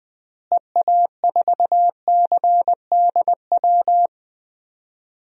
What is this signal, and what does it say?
Morse code, heard as a keyed tone, about 20 wpm, 700 Hz. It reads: EA4CDW
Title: DXCC-100 Call Signs - 6 Characters - Repeated 3x - 20wpm